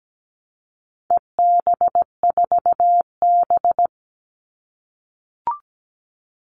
Morse code, heard as a keyed tone, about 17 wpm, 700 Hz. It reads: EB4B
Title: All Cut Numbers - 4-Digit 17wpm